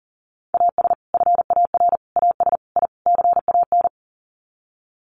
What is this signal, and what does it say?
Morse code, heard as a keyed tone, about 40 wpm, 700 Hz. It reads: AS FAR AS I CAN
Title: Sets of 5 Words Encore 40wpm